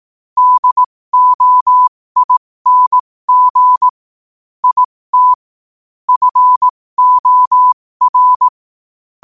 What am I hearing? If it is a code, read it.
Morse code, heard as a keyed tone, about 18 wpm, spaced as for 15 wpm, 1000 Hz. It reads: DOING IT FOR
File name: qotdfiles / 15wpm